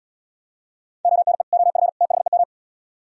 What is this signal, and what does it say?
Morse code, heard as a keyed tone, about 76 wpm, 700 Hz. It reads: ONE OF THEM